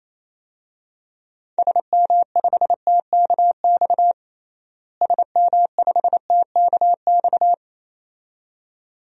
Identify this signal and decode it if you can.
Morse code, heard as a keyed tone, about 27 wpm, 700 Hz. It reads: SM5TKX SM5TKX
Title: DXCC-100 Call Signs - 6 Characters - Repeated 2x - 28wpm